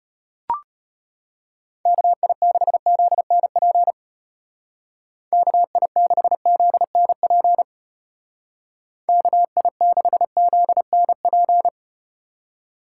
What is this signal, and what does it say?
Morse code, heard as a keyed tone, about 33 wpm, 700 Hz. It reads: KI6ZNP KI6ZNP KI6ZNP